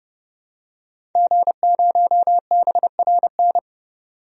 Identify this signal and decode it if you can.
Morse code, heard as a keyed tone, about 30 wpm, 700 Hz. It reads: G0BRN
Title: DXCC-100 Call Signs - 5 Characters - 30wpm